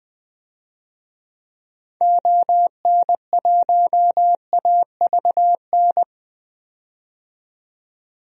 Morse code, heard as a keyed tone, about 20 wpm, 700 Hz. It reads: ON1AVN
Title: DXCC-100 Call Signs - 6 Characters - 20wpm